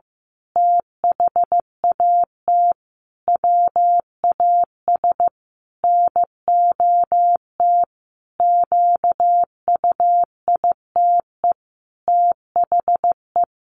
Morse code, heard as a keyed tone, about 15 wpm, 700 Hz. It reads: THAT WAS NOT QUITE THE
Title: A Sound Of Thunder 15wpm